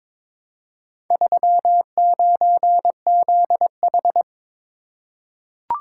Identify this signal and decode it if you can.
Morse code, heard as a keyed tone, about 22 wpm, 700 Hz. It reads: 39ZH
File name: practice / 22wpm / Four Letter-Number Combinations - 22wpm